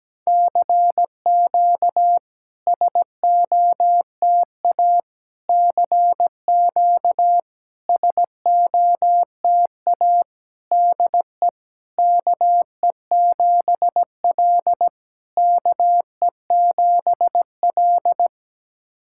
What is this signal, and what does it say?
Morse code, heard as a keyed tone, about 17 wpm, 700 Hz. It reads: CQ SOTA CQ SOTA DE KE7L KE7L
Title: SOTA QSO Practice 17wpm